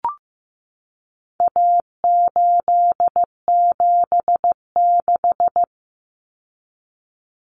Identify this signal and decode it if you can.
Morse code, heard as a keyed tone, about 15 wpm, 700 Hz. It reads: A876
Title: Common Cut Numbers - 4-Digit 15wpm